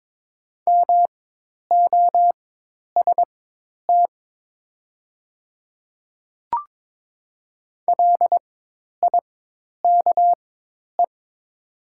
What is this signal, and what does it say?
Morse code, heard as a keyed tone, about 22 wpm, spaced as for 10 wpm, 700 Hz. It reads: MOST LIKE
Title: 4x Character Spacing - Top 100 Words - 22wpm